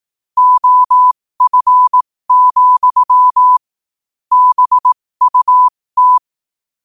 Morse code, heard as a keyed tone, about 18 wpm, spaced as for 15 wpm, 1000 Hz. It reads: OF, BUT
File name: qotdfiles / 15wpm